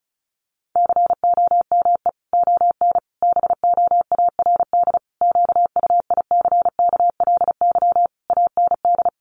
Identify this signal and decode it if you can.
Morse code, heard as a keyed tone, about 34 wpm, 700 Hz. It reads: COME ON BOARD QUICKLY AND
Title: Sentences from Top 1300 Words 35wpm